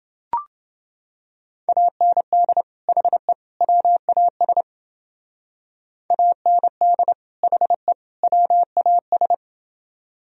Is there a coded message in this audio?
Yes: Morse code from 1.7 to 9.3 s, heard as a keyed tone, about 29 wpm, 700 Hz. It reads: AND HE WAS AND HE WAS